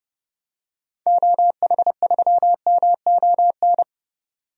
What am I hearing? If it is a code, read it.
Morse code, heard as a keyed tone, about 30 wpm, 700 Hz. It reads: OH3MON
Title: DXCC-100 Call Signs - 6 Characters - 30wpm